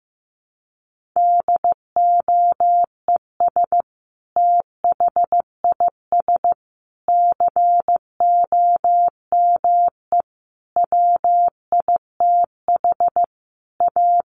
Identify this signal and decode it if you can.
Morse code, heard as a keyed tone, about 15 wpm, 700 Hz. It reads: DOES THIS COME WITH A
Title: Sentences from Top 1400 Words 15wpm